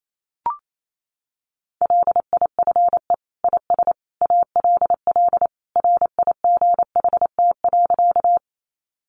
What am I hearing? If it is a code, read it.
Morse code, heard as a keyed tone, about 28 wpm, 700 Hz. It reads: LIFE IS ALL RIGHT.
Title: Sentences from Top 300 Words 28wpm